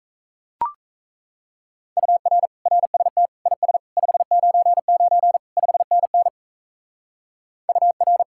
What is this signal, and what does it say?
Morse code, heard as a keyed tone, about 34 wpm, 700 Hz. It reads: UR RST IS 599 5NN UR
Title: Speed-Racing Common QSO Phrases - 28wpm